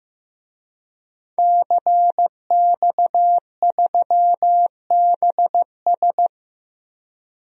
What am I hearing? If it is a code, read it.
Morse code, heard as a keyed tone, about 15 wpm, 700 Hz. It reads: CX3BS
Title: DXCC-100 Call Signs - 5 Characters - 15wpm